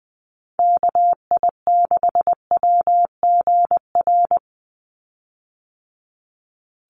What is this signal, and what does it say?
Morse code, heard as a keyed tone, about 20 wpm, 700 Hz. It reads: KI6WGR
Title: USA Call Signs 20wpm